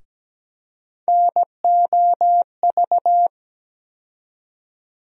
Morse code, heard as a keyed tone, about 17 wpm, 700 Hz. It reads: NOV
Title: USA Call Sign Postfixes - 3 Letters 17wpm